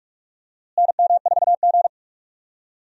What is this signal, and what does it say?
Morse code, heard as a keyed tone, about 43 wpm, 700 Hz. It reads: NM4G
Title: USA Call Signs - 4 Characters - Repeated 2x - 45wpm